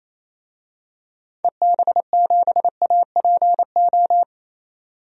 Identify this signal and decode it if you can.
Morse code, heard as a keyed tone, about 28 wpm, 700 Hz. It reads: EB7APO
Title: DXCC-100 Call Signs - 6 Characters - 28wpm